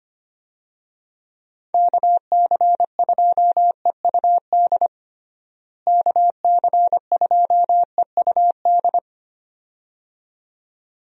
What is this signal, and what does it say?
Morse code, heard as a keyed tone, about 25 wpm, 700 Hz. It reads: KC2EUD KC2EUD
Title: USA Call Signs - 6 Characters - Repeated 2x - 25wpm